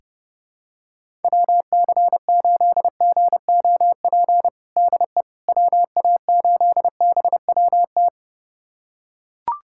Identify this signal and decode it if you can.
Morse code, heard as a keyed tone, about 32 wpm, 700 Hz. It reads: WC8GOP DE WA8BWT
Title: USA Callsign DE Callsign 30wpm